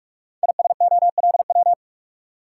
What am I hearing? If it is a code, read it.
Morse code, heard as a keyed tone, about 48 wpm, 700 Hz. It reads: ISOLW
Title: Code Groups - 5 Letter:Numbers - 45wpm